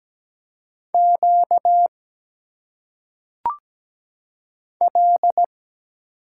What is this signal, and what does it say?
Morse code, heard as a keyed tone, about 17 wpm, 700 Hz. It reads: Q L
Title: Single Letters - Phonetic Pronunciation 17wpm